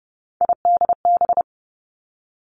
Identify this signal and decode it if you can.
Morse code, heard as a keyed tone, about 30 wpm, 700 Hz. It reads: IDB
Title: Three Challenging Letter-Number Combinations 30wpm